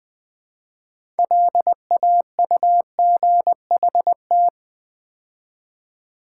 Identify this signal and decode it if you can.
Morse code, heard as a keyed tone, about 21 wpm, 700 Hz. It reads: LAUGHT
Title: Build A Word - 2 Letters At A Time - 8-Letter Words - 20wpm